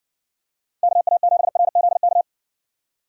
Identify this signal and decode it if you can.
Morse code, heard as a keyed tone, about 60 wpm, 700 Hz. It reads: KA7RZK